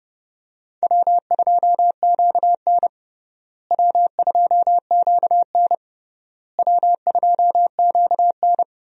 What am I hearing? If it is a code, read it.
Morse code, heard as a keyed tone, about 30 wpm, 700 Hz. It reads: W2QN W2QN W2QN
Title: USA Call Signs - 4 Characters - Repeated 3x - 30wpm